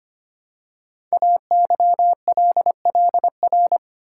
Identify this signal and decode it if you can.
Morse code, heard as a keyed tone, about 25 wpm, 700 Hz. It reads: AYLLR
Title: Code Groups - 5 Letters - 25wpm